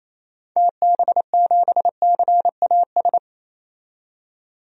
Morse code, heard as a keyed tone, about 28 wpm, 700 Hz. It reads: TB7CAS
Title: DXCC-100 Call Signs - 6 Characters - 28wpm